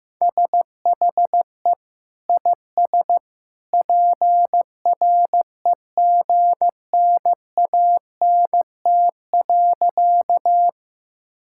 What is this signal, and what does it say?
Morse code, heard as a keyed tone, about 15 wpm, 700 Hz. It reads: SHE IS PREGNANT.
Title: Sentences from Top 1000 Words - Review Part 6 - 15wpm